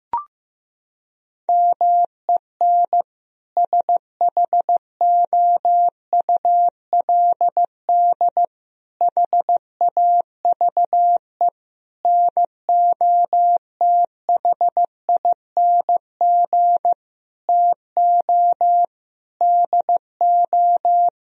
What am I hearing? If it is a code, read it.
Morse code, heard as a keyed tone, about 15 wpm, 700 Hz. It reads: MEN SHOULD HAVE NOTHING TO DO
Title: Sentences from Top 1000 Words - Review Part 3 - 15wpm